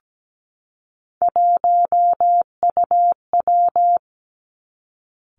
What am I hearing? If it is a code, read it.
Morse code, heard as a keyed tone, about 17 wpm, 700 Hz. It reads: 1UW
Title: Three Challenging Letter-Number Combinations 17wpm